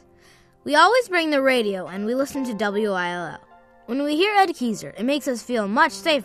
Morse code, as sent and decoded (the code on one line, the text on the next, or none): none